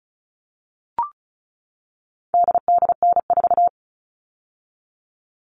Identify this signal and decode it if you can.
Morse code, heard as a keyed tone, about 34 wpm, 700 Hz. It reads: DDN4
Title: All Cut Numbers - 4-Digit 35wpm